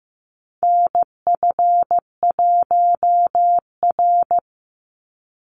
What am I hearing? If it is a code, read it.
Morse code, heard as a keyed tone, about 15 wpm, 700 Hz. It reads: NF1R
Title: NA SOTA Chaser Callsigns 15wpm